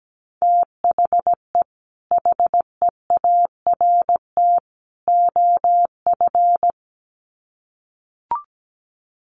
Morse code, heard as a keyed tone, about 17 wpm, 700 Hz. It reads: THE HEART OF
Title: Sets of 3 Words 17wpm